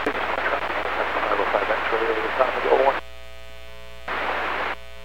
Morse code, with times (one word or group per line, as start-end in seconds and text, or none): none